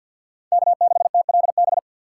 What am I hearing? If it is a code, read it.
Morse code, heard as a keyed tone, about 48 wpm, 700 Hz. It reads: K6TLB
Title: USA Call Signs - 5 Characters - 6x Character Spacing - 50wpm